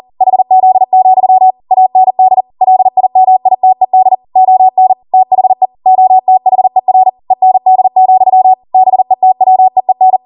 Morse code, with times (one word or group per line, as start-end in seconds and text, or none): 0.2-1.5 HZ,
1.7-2.4 AND
2.6-4.2 LIMITED
4.4-4.9 ON
5.1-5.7 THE
5.9-7.1 OTHER
7.3-8.5 END,
8.7-10.2 BETWEEN